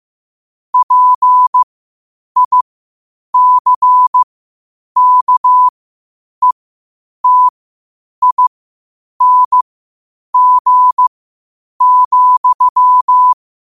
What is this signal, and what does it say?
Morse code, heard as a keyed tone, about 15 wpm, spaced as for 8 wpm, 1000 Hz. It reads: PICKETING,
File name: qotdfiles / 8wpm